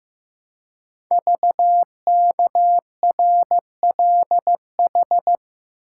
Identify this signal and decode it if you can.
Morse code, heard as a keyed tone, about 15 wpm, 700 Hz. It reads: VKRLH
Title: Code Groups - 5 Letters - 15wpm